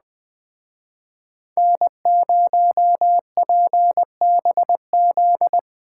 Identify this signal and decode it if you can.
Morse code, heard as a keyed tone, about 19 wpm, 700 Hz. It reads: N0PBZ